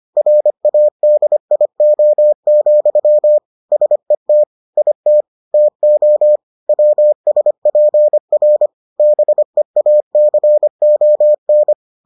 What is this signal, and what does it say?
Morse code, heard as a keyed tone, about 25 wpm, 600 Hz. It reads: RADIO, SET IT TO WSPR BEACON